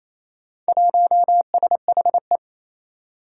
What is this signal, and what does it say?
Morse code, heard as a keyed tone, about 28 wpm, 700 Hz. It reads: 1SHE
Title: Four Challenging Letter-Number Combinations 28wpm